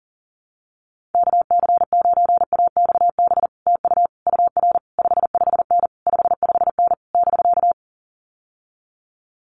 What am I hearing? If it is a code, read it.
Morse code, heard as a keyed tone, about 40 wpm, 700 Hz. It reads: KC9AXB TU UR 55N 55N <BK>
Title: Common QSO Phrases Encore 40wpm